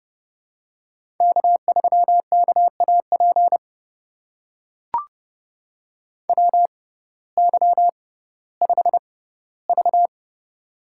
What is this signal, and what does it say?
Morse code, heard as a keyed tone, about 30 wpm, spaced as for 17 wpm, 700 Hz. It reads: K3KAP WY5V
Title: USA Call Signs - 6x Character Spacing - 30wpm